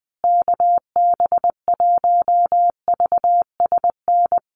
Morse code, heard as a keyed tone, about 20 wpm, 700 Hz. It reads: KB1VSN
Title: USA Call Signs Repeated 2x 20wpm